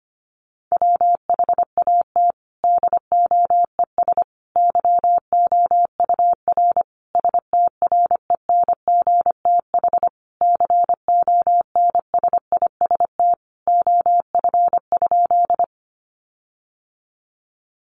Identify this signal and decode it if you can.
Morse code, heard as a keyed tone, about 25 wpm, 700 Hz. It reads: WHAT DOES YOUR STRENGTH CONSIST OF?
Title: Sentences from Top 1400 Words 25wpm